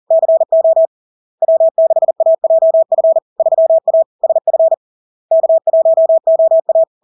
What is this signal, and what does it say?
Morse code, heard as a keyed tone, about 40 wpm, 650 Hz. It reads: CO W6AJF 3A SF K1OA
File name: LARK-FD-2022-Log-in-CW-40wpm-650-Hz